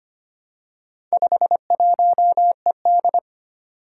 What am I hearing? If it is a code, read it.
Morse code, heard as a keyed tone, about 25 wpm, 700 Hz. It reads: 51ED
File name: practice / 25wpm / Four Letter-Number Combinations - 25wpm